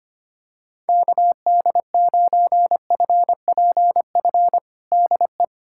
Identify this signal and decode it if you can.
Morse code, heard as a keyed tone, about 25 wpm, 700 Hz. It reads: KD9FPF DE
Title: USA Callsign DE Callsign - 6 Characters - 25wpm